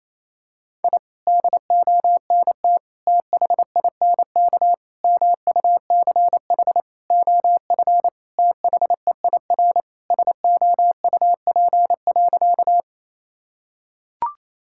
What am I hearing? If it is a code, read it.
Morse code, heard as a keyed tone, about 27 wpm, 690 Hz. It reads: I DONT THINK MUCH OF THEIR SOUP.
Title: Sentences from Top 1000 Words - Review Part 6 - 28wpm